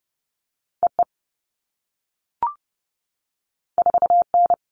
Morse code, heard as a keyed tone, about 30 wpm, 700 Hz. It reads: E E 4 N
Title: All Cut Numbers - 2-Digit 30wpm